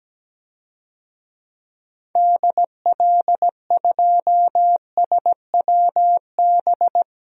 Morse code, heard as a keyed tone, about 17 wpm, 700 Hz. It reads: DL2SWB
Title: DXCC-100 Call Signs - 6 Characters - 17wpm